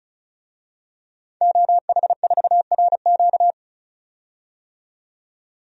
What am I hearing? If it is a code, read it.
Morse code, heard as a keyed tone, about 34 wpm, 700 Hz. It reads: OH4RQ
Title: DXCC-100 Call Signs - 5 Characters - 35wpm